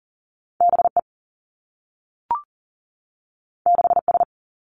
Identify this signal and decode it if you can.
Morse code, heard as a keyed tone, about 40 wpm, 700 Hz. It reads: B E 6 S
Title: Two Challenging Letter-Number Combinations 40wpm